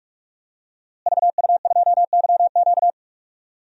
Morse code, heard as a keyed tone, about 46 wpm, 700 Hz. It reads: VU2YQ
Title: DXCC-100 Call Signs - 5 Characters - Repeated 3x - 45wpm